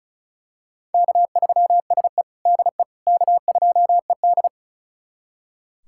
Morse code, heard as a keyed tone, about 36 wpm, 700 Hz. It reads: K3SE DE K2ED
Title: USA Callsign DE Callsign - 4 Characters - 35wpm